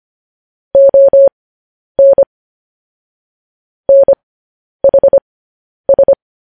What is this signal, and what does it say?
Morse code, heard as a keyed tone, about 25 wpm, spaced as for 10 wpm, 550 Hz. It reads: ON NHS